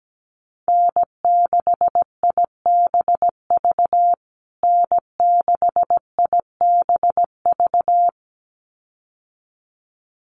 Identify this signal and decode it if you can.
Morse code, heard as a keyed tone, about 17 wpm, 700 Hz. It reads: N6IBV N6IBV
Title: USA Call Signs Repeated 2x 17wpm